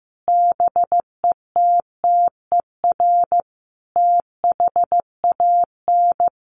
Morse code, heard as a keyed tone, about 15 wpm, 700 Hz. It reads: BETTER THAN
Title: Sentences from Top 1200 Words 15wpm